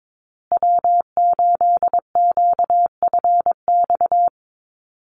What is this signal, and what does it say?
Morse code, heard as a keyed tone, about 22 wpm, 700 Hz. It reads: W8QFX
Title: USA Call Signs Repeated 2x 22wpm